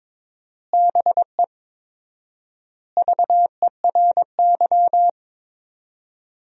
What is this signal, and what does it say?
Morse code, heard as a keyed tone, about 22 wpm, 700 Hz. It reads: BE VERY
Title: 4x Word-Spacing - Sentences from Top 200 Words - 22wpm